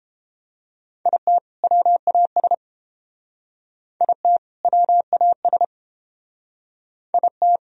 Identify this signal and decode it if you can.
Morse code, heard as a keyed tone, about 30 wpm, 700 Hz. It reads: IT WAS IT WAS IT